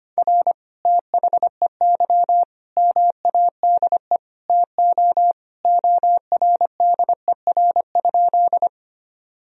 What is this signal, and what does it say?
Morse code, heard as a keyed tone, about 25 wpm, 700 Hz. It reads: R THEY MADE TO ORDER?
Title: Sentences from Top 400 Words and Abbreviations 25wpm